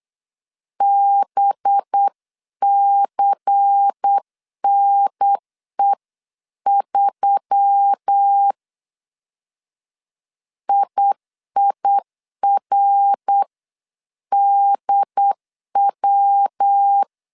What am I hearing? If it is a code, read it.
Morse code, heard as a keyed tone, about 8 wpm, 800 Hz. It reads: BCNE3 IIRDW
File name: 8wpm